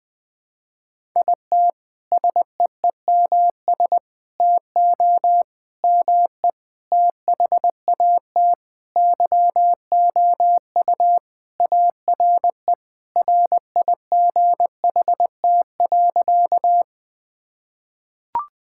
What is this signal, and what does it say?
Morse code, heard as a keyed tone, about 20 wpm, 700 Hz. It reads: IT SEEMS TO ME THAT YOU ARE RIGHT.
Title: Sentences with 5-Word Phrases - Spoken Beginning - 20wpm